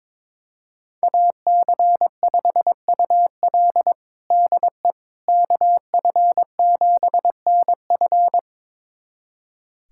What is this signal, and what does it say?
Morse code, heard as a keyed tone, about 21 wpm, 700 Hz. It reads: AC5UL DE KF7NF
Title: USA Callsign DE Callsign 22wpm